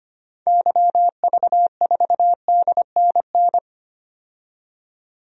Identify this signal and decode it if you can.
Morse code, heard as a keyed tone, about 25 wpm, 690 Hz. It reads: YV4DNN
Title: DXCC-100 Call Signs - 6 Characters - 25wpm